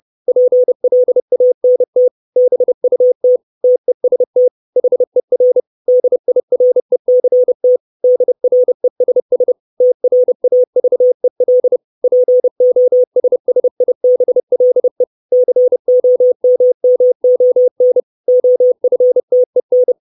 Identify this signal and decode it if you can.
Morse code, heard as a keyed tone, about 30 wpm, 500 Hz. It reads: PLANT BUT TEST HER DIRECT DRESS TRAVEL POSSIBLE COMMON OFTEN
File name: 632 words / 632 Words 30WPM